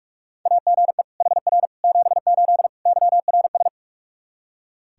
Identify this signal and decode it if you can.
Morse code, heard as a keyed tone, about 45 wpm, 700 Hz. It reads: AGE HR 78 YRS
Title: K1USN SST QSO Practice 45wpm